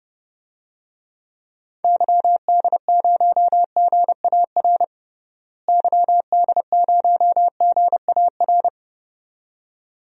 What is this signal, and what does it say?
Morse code, heard as a keyed tone, about 32 wpm, 700 Hz. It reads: YD0GAR YD0GAR
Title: DXCC-100 Call Signs - 6 Characters - Repeated 2x - 30wpm